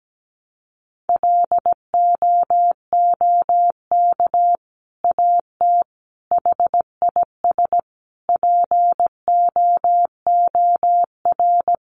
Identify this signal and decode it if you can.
Morse code, heard as a keyed tone, about 17 wpm, 700 Hz. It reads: LOOK AT HIS POOR
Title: Sentences from Top 1300 Words 17wpm